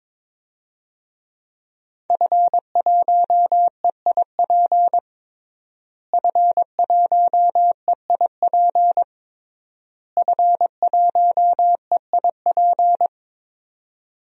Morse code, heard as a keyed tone, about 22 wpm, 700 Hz. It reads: F1EIP F1EIP F1EIP
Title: DXCC-100 Call Signs - 5 Characters - Repeated 3x - 22wpm